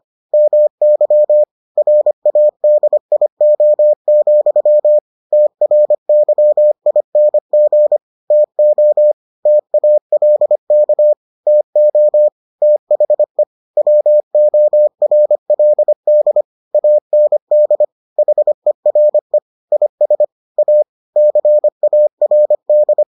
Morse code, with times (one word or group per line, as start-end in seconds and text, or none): 0.3-1.4 MY
1.8-5.0 RADIO,
5.3-8.0 TRYING
8.3-9.1 TO
9.5-11.1 TALK
11.5-12.3 TO
12.6-13.4 THE
13.8-16.4 WORLD
16.7-17.8 AND
18.2-19.4 HERE
19.7-20.2 IS
20.6-20.8 A
21.2-23.0 CARD